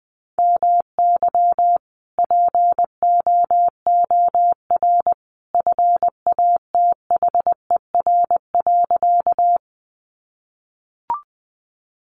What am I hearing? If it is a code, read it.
Morse code, heard as a keyed tone, about 20 wpm, 700 Hz. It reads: MY POOR FATHER.
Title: Sentences from Top 1000 Words 20wpm